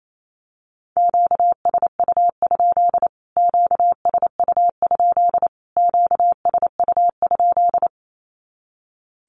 Morse code, heard as a keyed tone, about 28 wpm, 700 Hz. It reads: QSU? QSU? QSU?
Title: Q-Codes Repeated 3x 28wpm